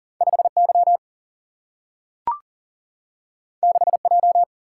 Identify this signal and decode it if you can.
Morse code, heard as a keyed tone, about 39 wpm, 700 Hz. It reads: 5Y 6J
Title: Single-Letter Single-Number 40wpm